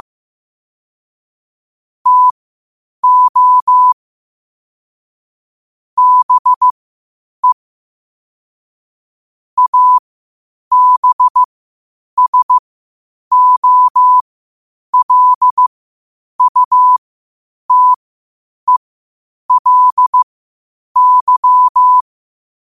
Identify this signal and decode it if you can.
Morse code, heard as a keyed tone, about 15 wpm, spaced as for 8 wpm, 1000 Hz. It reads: TO BE ABSOLUTELY
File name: qotdfiles / 8wpm